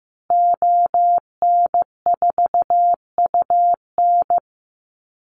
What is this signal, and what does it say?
Morse code, heard as a keyed tone, about 14 wpm, 700 Hz. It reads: ON4UN
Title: Call Signs Repeated 2x - 15wpm